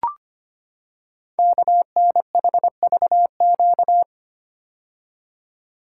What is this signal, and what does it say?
Morse code, heard as a keyed tone, about 25 wpm, 700 Hz. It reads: KNHVQ
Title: Code Groups - 5 Letters - 25wpm